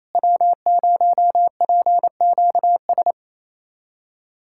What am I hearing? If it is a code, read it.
Morse code, heard as a keyed tone, about 28 wpm, 700 Hz. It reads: W0PQS
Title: USA Callsign DE Callsign - 5 Characters - 28wpm